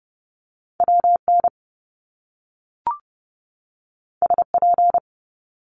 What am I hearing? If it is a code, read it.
Morse code, heard as a keyed tone, about 30 wpm, 700 Hz. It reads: WN SP
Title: Top Two Letter Combinations in Words 30wpm